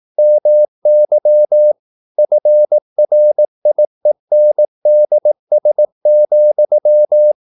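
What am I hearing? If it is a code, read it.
Morse code, heard as a keyed tone, about 18 wpm, 600 Hz. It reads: MY FRIENDS,
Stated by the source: Morse practice at 18 wpm, English text, from CW med Gnister.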